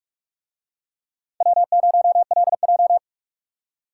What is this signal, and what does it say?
Morse code, heard as a keyed tone, about 42 wpm, 700 Hz. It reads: W0LJ